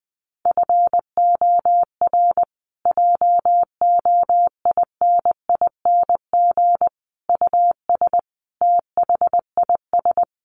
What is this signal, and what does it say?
Morse code, heard as a keyed tone, about 20 wpm, 700 Hz. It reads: FOR JOINING US THIS
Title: Sets of 5 Words Encore 20wpm